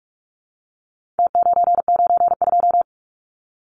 Morse code, heard as a keyed tone, about 44 wpm, 700 Hz. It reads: T992
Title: Numbers - Contest Style - 4-Digit 45wpm